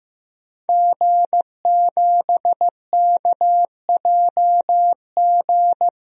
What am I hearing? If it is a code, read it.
Morse code, heard as a keyed tone, about 15 wpm, 700 Hz. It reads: G7KJG
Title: DXCC-100 Call Signs - 5 Characters - Repeated 3x - 15wpm